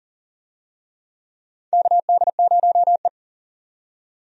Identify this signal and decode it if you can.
Morse code, heard as a keyed tone, about 40 wpm, 700 Hz. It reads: KD0E
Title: USA Call Signs - 4 Characters - 40wpm